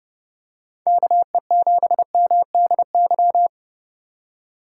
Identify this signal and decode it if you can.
Morse code, heard as a keyed tone, about 30 wpm, 700 Hz. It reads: KE7MDY